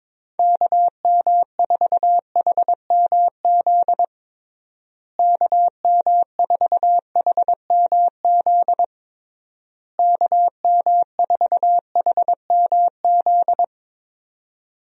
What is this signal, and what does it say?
Morse code, heard as a keyed tone, about 21 wpm, 700 Hz. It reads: KM4HMZ KM4HMZ KM4HMZ
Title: USA Call Signs - 6 Characters - Repeated 3x - 22wpm